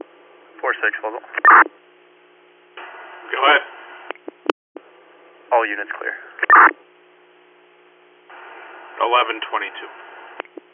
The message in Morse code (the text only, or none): none